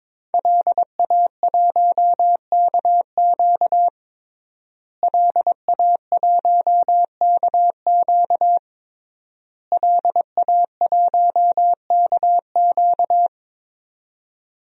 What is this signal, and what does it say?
Morse code, heard as a keyed tone, about 22 wpm, 700 Hz. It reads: LA1KQ LA1KQ LA1KQ
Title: DXCC-100 Call Signs - 5 Characters - Repeated 3x - 22wpm